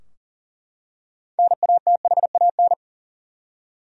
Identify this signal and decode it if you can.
Morse code, heard as a keyed tone, about 40 wpm, 700 Hz. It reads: NATHAN
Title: USA Common Names - 40wpm